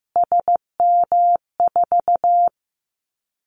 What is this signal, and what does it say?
Morse code, heard as a keyed tone, about 15 wpm, 700 Hz. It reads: SM4
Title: Callsign Prefixes for Top 25 DXCC - Repeated 3x - 15wpm